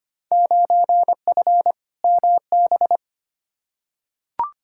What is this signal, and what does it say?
Morse code, heard as a keyed tone, about 25 wpm, 700 Hz. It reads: 9F MB
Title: Field Day Exchanges Encore - 25wpm